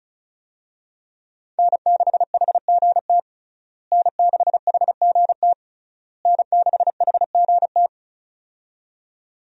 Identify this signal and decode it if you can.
Morse code, heard as a keyed tone, about 36 wpm, 700 Hz. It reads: N6HGT N6HGT N6HGT
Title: USA Call Signs - 5 Characters - Repeated 3x - 35wpm